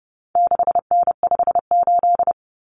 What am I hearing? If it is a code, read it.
Morse code, heard as a keyed tone, about 30 wpm, 700 Hz. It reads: 6N58
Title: Common Cut Numbers - 4-Digit 30wpm